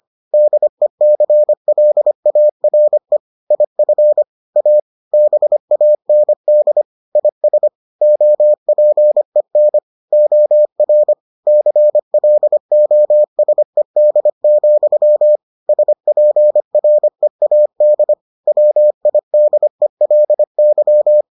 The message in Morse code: DECLARE IF A BAND IS OPEN OR CLOSED, SPREAD WIDELY

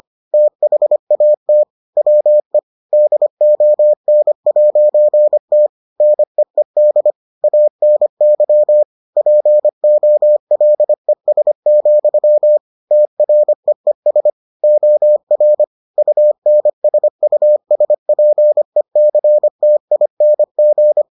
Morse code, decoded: THAT WE DON'T NEED ANY POLES, TREES OR UNSUSPECTING

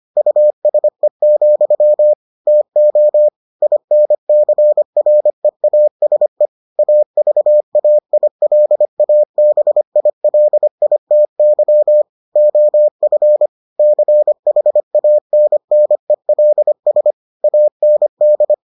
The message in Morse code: USE, TO INCREASE AVAILABILITY OF CHANNELS AND

